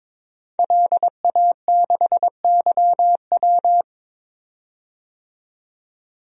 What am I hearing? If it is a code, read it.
Morse code, heard as a keyed tone, about 21 wpm, 700 Hz. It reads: LA6YW
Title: DXCC-100 Call Signs - 5 Characters - 22wpm